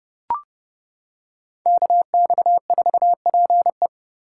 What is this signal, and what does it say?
Morse code, heard as a keyed tone, about 28 wpm, 700 Hz. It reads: KX4PE